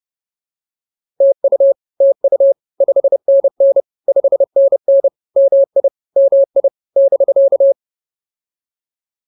Morse code, heard as a keyed tone, about 30 wpm, 550 Hz. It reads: TU TU 5NN 5NN MI MI <BK>
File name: POTA110 30